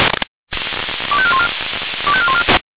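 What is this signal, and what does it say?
Five-tone selective call (ZVEI), heard as quick runs of five tones, tones of about 70 ms. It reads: 25414 25414
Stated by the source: SDR-received ZVEI five-tone selective call, code 25202.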